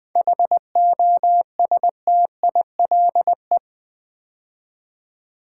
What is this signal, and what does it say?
Morse code, heard as a keyed tone, about 20 wpm, 700 Hz. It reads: HOSTILE
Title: 7 Letter Words - First 3 Letters Given - 20wpm